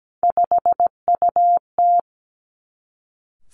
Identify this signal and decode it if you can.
Morse code, heard as a keyed tone, about 17 wpm, 700 Hz. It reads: 5UT
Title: Three Challenging Letter-Number Combinations 17wpm